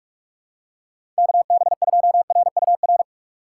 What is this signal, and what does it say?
Morse code, heard as a keyed tone, about 46 wpm, 700 Hz. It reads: KB2RUR
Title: USA Call Signs - 4x Character Spacing - 45wpm